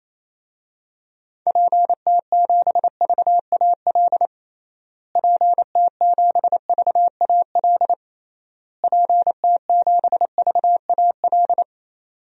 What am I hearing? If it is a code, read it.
Morse code, heard as a keyed tone, about 28 wpm, 700 Hz. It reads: PT7VAL PT7VAL PT7VAL
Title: DXCC-100 Call Signs - 6 Characters - Repeated 3x - 28wpm